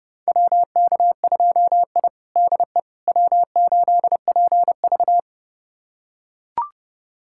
Morse code, heard as a keyed tone, about 29 wpm, 700 Hz. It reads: WK2I DE W8PV
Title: USA Callsign DE Callsign - 4 Characters - 30wpm